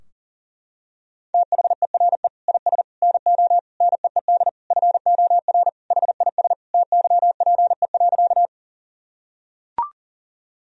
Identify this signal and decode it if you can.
Morse code, heard as a keyed tone, about 40 wpm, 700 Hz. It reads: THERE IS NO NEED FOR HIS TYPE.
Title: Sentences from Top 1000 Words - Review Part 3 - 40wpm